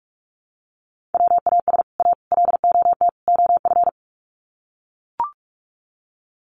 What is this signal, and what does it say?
Morse code, heard as a keyed tone, about 44 wpm, 700 Hz. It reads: WAS A LOT OF
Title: Sets of 4 Words 45wpm